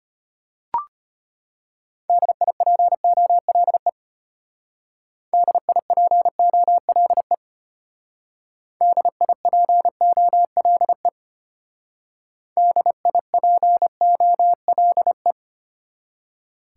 Morse code, heard as a keyed tone, about 33 wpm, 700 Hz. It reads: DIPOLE DIPOLE DIPOLE DIPOLE